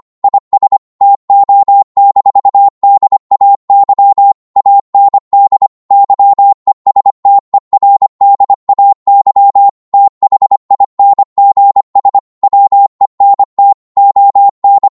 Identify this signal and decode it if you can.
Morse code, heard as a keyed tone, about 25 wpm, 800 Hz. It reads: IS TO-DAY AND YESTERDAY THINGS WENT ON